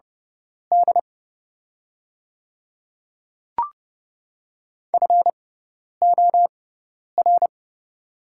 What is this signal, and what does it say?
Morse code, heard as a keyed tone, about 30 wpm, spaced as for 10 wpm, 700 Hz. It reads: D FOR
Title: 6x Character Spacing - Top 100 Words - 30wpm